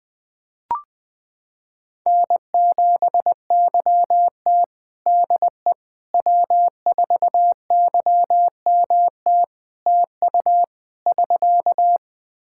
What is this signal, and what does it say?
Morse code, heard as a keyed tone, about 21 wpm, 700 Hz. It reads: N7YT DE W4YMT TU <SK>
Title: K1USN SST QSO Practice 20wpm